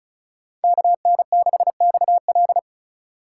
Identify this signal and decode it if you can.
Morse code, heard as a keyed tone, about 34 wpm, 700 Hz. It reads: KN6XL
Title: USA Call Signs - 5 Characters - 3x Character Spacing - 35wpm